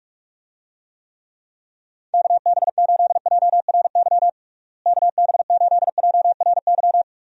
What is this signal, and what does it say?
Morse code, heard as a keyed tone, about 42 wpm, 700 Hz. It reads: KB8JRY KB8JRY
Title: USA Call Signs - 6 Characters - Repeated 2x - 45wpm